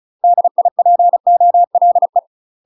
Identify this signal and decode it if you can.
Morse code, heard as a keyed tone, about 34 wpm, 700 Hz. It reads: DIPOLE
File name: UK MASTER QSOs TRAINING / 35WPM / UK MASTER QSOs 0016